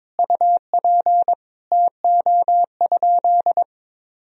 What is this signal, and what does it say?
Morse code, heard as a keyed tone, about 22 wpm, 700 Hz. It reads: UP TO?
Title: Sentences from Top 1000 Words - Review Part 1 - 22wpm